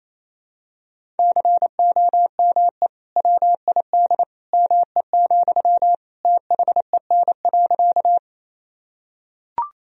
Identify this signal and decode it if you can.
Morse code, heard as a keyed tone, about 28 wpm, 700 Hz. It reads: COME WID ME, THEN.
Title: Sentences from Top 200 Words and Abbreviations 28wpm